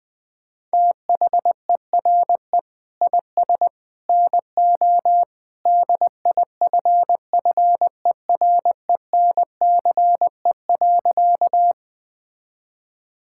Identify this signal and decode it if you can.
Morse code, heard as a keyed tone, about 20 wpm, 700 Hz. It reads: THERE IS NO DIFFERENCE.
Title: Sentences from Top 1000 Words - Review Part 5 - 20wpm